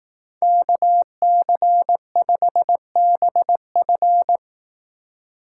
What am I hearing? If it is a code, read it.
Morse code, heard as a keyed tone, about 19 wpm, 700 Hz. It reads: KC5BF
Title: Speed Racing - USA 2x2 Call Signs 15wpm